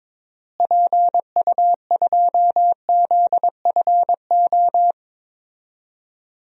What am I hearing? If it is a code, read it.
Morse code, heard as a keyed tone, about 22 wpm, 700 Hz. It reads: PU2ZFO
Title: DXCC-100 Call Signs - 6 Characters - Repeated 2x - 22wpm